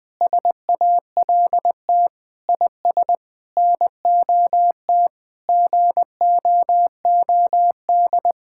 Morse code, heard as a keyed tone, about 20 wpm, 700 Hz. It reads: SALT IS NOT GOOD
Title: Sentences from Top 1000 Words - Review Part 5 - 20wpm